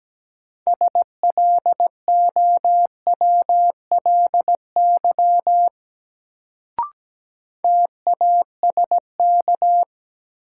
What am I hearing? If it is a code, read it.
Morse code, heard as a keyed tone, about 17 wpm, 700 Hz. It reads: SLOWLY TASK
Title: Sentences from Top 1100 Words - New Word Drill Alt - 17wpm